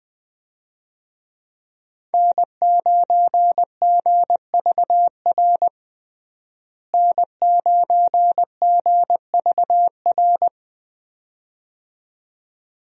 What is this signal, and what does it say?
Morse code, heard as a keyed tone, about 20 wpm, 700 Hz. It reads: N9GVR N9GVR
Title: USA Call Signs - 5 Characters - Repeated 2x - 20wpm